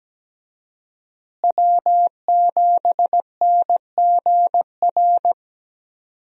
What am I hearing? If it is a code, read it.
Morse code, heard as a keyed tone, about 17 wpm, 700 Hz. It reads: W7NGR
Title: USA Call Signs - 5 Characters - Repeated 2x - 17wpm